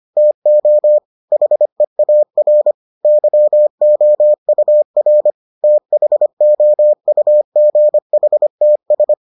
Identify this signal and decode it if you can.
Morse code, heard as a keyed tone, about 25 wpm, 600 Hz. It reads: TO HEAR YOUR THOUGHTS